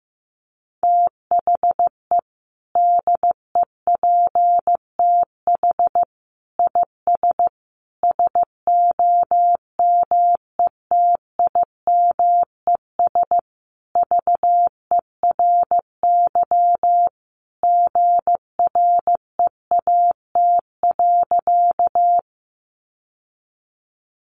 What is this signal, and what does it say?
Morse code, heard as a keyed tone, about 14 wpm, 700 Hz. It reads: THE DEPTH IS SOMETIMES VERY GREAT.
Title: Sentences from Top 1200 Words 15wpm